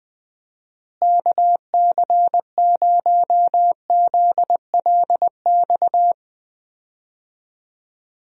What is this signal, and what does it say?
Morse code, heard as a keyed tone, about 20 wpm, 700 Hz. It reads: KC0ZLX
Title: USA Call Signs - 6 Characters - Repeated 2x - 20wpm